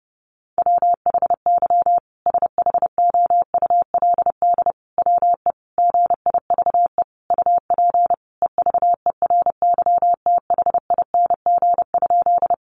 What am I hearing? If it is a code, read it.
Morse code, heard as a keyed tone, about 30 wpm, 700 Hz. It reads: WHY SHOULD WE GIVE UP EVERYTHING?
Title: Sentences from Top 700 Words 30wpm